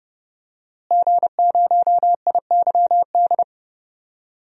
Morse code, heard as a keyed tone, about 30 wpm, 700 Hz. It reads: G0IYD
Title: DXCC-100 Call Signs - 5 Characters - Repeated 3x - 30wpm